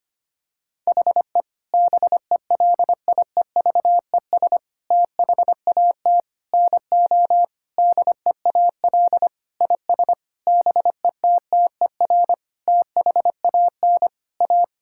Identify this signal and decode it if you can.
Morse code, heard as a keyed tone, about 25 wpm, 700 Hz. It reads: HE BELIEVES THAT NO DEAL IS BETTER THAN A
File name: practice / 25wpm / Sentences with 9-Word Phrases - 25wpm